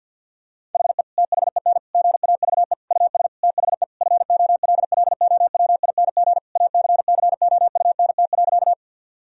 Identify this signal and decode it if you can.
Morse code, heard as a keyed tone, about 50 wpm, 700 Hz. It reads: HE THEN GAVE US THE FOLLOWING ACCOUNT.
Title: Sentences from Top 1000 Words - Review Part 4 - 50wpm